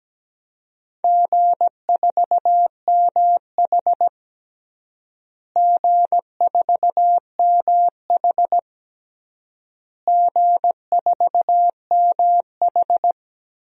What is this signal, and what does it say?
Morse code, heard as a keyed tone, about 17 wpm, 700 Hz. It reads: G4MH G4MH G4MH
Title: DXCC-100 Call Signs - 4 Characters - Repeated 3x - 17wpm